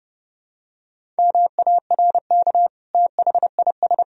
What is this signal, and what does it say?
Morse code, heard as a keyed tone, about 29 wpm, 700 Hz. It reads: MARK THIS
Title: Sentences from Top 1000 Words - Review Part 1 - 30wpm